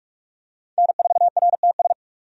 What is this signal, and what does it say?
Morse code, heard as a keyed tone, about 46 wpm, 700 Hz. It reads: N4RTS
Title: USA Call Signs - 5 Characters - 45wpm